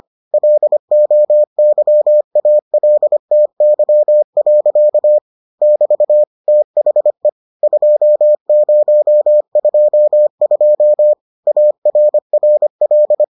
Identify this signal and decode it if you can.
Morse code, heard as a keyed tone, about 25 wpm, 600 Hz. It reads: LOYALTY. <BT> THE 2022 ARRL